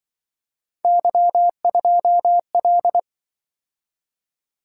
Morse code, heard as a keyed tone, about 25 wpm, 700 Hz. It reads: Y2L